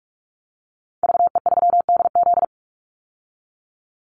Morse code, heard as a keyed tone, about 44 wpm, 700 Hz. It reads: VE3DZ
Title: Call Signs Repeated 3x - 45wpm